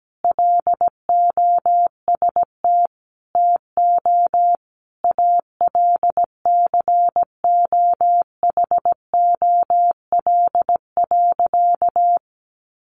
Morse code, heard as a keyed tone, about 17 wpm, 700 Hz. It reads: LOST TO ALCOHOL.
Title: Sentences from Top 1400 Words 17wpm